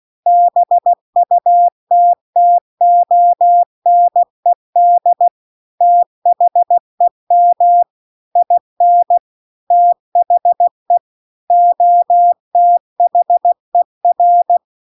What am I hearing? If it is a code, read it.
Morse code, heard as a keyed tone, about 16 wpm, 700 Hz. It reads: BUTTONED THEM IN THE OTHER